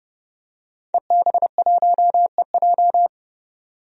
Morse code, heard as a keyed tone, about 30 wpm, 700 Hz. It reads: EB1EJ
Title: DXCC-100 Call Signs - 5 Characters - 30wpm